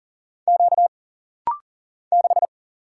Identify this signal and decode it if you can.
Morse code, heard as a keyed tone, about 39 wpm, 700 Hz. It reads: Q 6
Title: Single Letter-Number - Rapid-Fire 40wpm